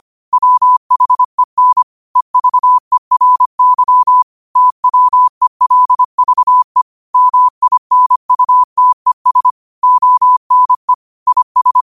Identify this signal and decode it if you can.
Morse code, heard as a keyed tone, about 25 wpm, 1000 Hz. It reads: WHEN EVERY TWELVE MINUTES ONE IS
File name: qotdfiles / 25wpm